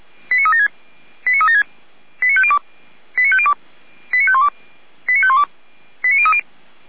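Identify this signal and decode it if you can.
Five-tone selective call (ZVEI), heard as quick runs of five tones, tones of about 70 ms. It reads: 89376 89376 89502 89502 89512 89512 89030
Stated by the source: ZVEI five-tone selective call code 89092, received by SDR.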